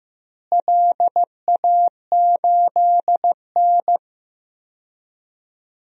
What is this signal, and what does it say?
Morse code, heard as a keyed tone, about 15 wpm, 700 Hz. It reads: LA8N
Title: DXCC-100 Call Signs - 4 Characters - Repeated 3x - 15wpm